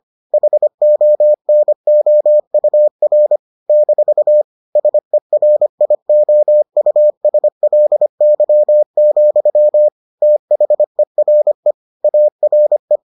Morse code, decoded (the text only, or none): HONOUR - SERIOUSLY, THERE ARE